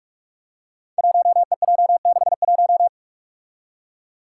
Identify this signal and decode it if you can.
Morse code, heard as a keyed tone, about 46 wpm, 700 Hz. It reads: 1EJ61
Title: Five Challenging Letter-Number Combinations 45wpm